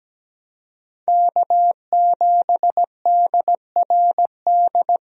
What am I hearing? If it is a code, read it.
Morse code, heard as a keyed tone, about 17 wpm, 700 Hz. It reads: K7DRD
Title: USA Call Signs - 5 Characters - 17wpm